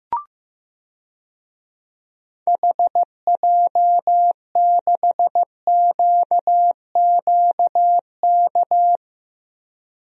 Morse code, heard as a keyed tone, about 15 wpm, 700 Hz. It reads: HJ6QQK
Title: DXCC-100 Call Signs - 6 Characters - Repeated 3x - 15wpm